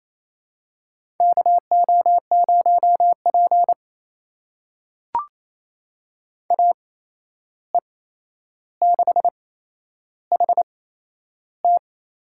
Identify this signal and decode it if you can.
Morse code, heard as a keyed tone, about 28 wpm, spaced as for 11 wpm, 700 Hz. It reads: KO0P AE6HT
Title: USA Call Signs - 8x Character Spacing - 28wpm